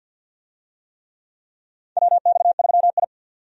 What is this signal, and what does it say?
Morse code, heard as a keyed tone, about 51 wpm, 700 Hz. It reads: WX3I
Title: USA Call Signs - 4 Characters - 50wpm